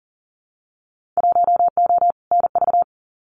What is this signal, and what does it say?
Morse code, heard as a keyed tone, about 41 wpm, 700 Hz. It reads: 1O NV
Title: Winter Field Day Exchanges Encore 40wpm